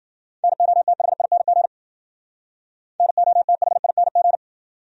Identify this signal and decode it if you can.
Morse code, heard as a keyed tone, about 52 wpm, 700 Hz. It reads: NOTHING NOTHING